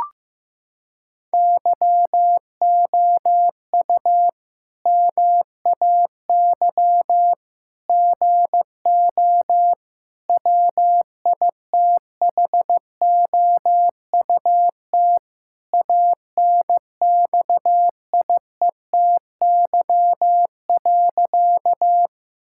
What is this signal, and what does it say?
Morse code, heard as a keyed tone, about 15 wpm, 700 Hz. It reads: YOU MAY GO WITHOUT ANXIETY.
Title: Sentences from Top 1600 Words 15wpm